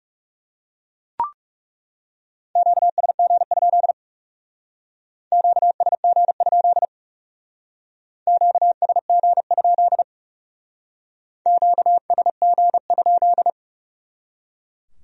Morse code, heard as a keyed tone, about 39 wpm, 700 Hz. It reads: QSG? QSG? QSG? QSG?